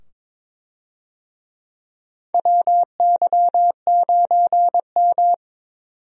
Morse code, heard as a keyed tone, about 21 wpm, 700 Hz. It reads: WY9M
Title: USA Call Signs - 4 Characters - 22wpm